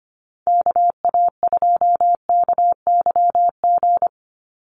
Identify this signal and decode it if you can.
Morse code, heard as a keyed tone, about 25 wpm, 690 Hz. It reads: KA2KYG
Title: USA Call Signs 25wpm